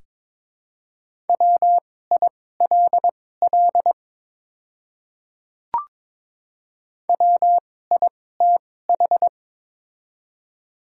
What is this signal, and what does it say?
Morse code, heard as a keyed tone, about 22 wpm, spaced as for 16 wpm, 700 Hz. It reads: WILL WITH
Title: 2x Character Spacing - Top 100 Words - 22wpm